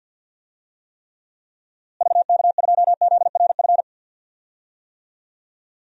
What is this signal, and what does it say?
Morse code, heard as a keyed tone, about 47 wpm, 700 Hz. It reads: VK2ZRF
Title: DXCC-100 Call Signs - 6 Characters - Repeated 2x - 50wpm